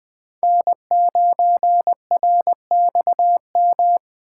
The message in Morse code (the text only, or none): N9RXM